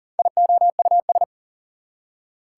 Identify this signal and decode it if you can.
Morse code, heard as a keyed tone, about 38 wpm, 700 Hz. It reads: IOUS
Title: Prefixes and Suffixes 40wpm